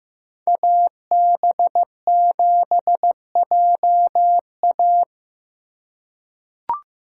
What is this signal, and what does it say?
Morse code, heard as a keyed tone, about 15 wpm, 700 Hz. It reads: AB7JA
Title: USA Call Signs - 5 Characters - 3x Character Spacing - 15wpm